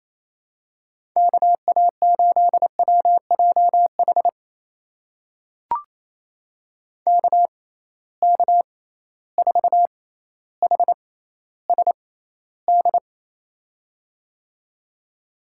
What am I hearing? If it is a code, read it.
Morse code, heard as a keyed tone, about 28 wpm, spaced as for 14 wpm, 700 Hz. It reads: KA8WJH KK4HSD
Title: USA Call Signs - 6 Characters - 6x Character Spacing - 28wpm